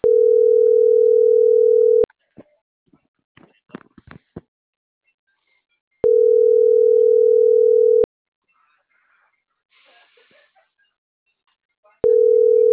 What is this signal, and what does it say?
Ringback tone (North America), 440 + 480 Hz.